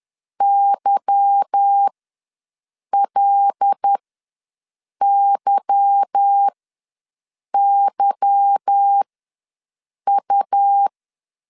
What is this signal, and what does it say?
Morse code, heard as a keyed tone, about 10 wpm, spaced as for 6 wpm, 800 Hz. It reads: YLYYU